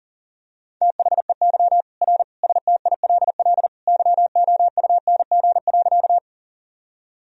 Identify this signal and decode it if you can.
Morse code, heard as a keyed tone, about 40 wpm, 700 Hz. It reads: THEY R STILL YOUNG.
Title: Sentences from Top 400 Words and Abbreviations 40wpm